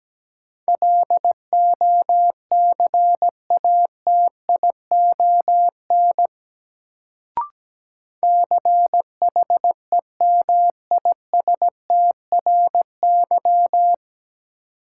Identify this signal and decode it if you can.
Morse code, heard as a keyed tone, about 17 wpm, 700 Hz. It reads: LOCATION CHEMISTRY
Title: Sentences from Top 1000 Words - Review Part 6 - New Word Drill - 17wpm